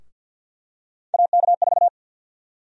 Morse code, heard as a keyed tone, about 47 wpm, 700 Hz. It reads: AK4
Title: USA Call Sign Prefixes - Repeated 2x - 50wpm